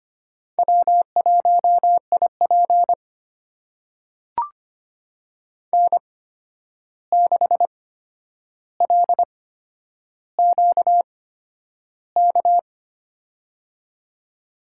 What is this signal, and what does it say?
Morse code, heard as a keyed tone, about 25 wpm, spaced as for 10 wpm, 700 Hz. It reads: W1IP N6LQK
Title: USA Call Signs - 8x Character Spacing - 25wpm